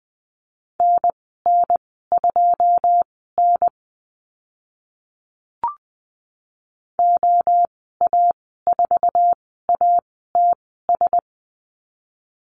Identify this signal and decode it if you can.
Morse code, heard as a keyed tone, about 20 wpm, spaced as for 15 wpm, 700 Hz. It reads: NN2N OA4ATS
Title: 082 Lesson 14g - Number 2 - Call Signs 20wpm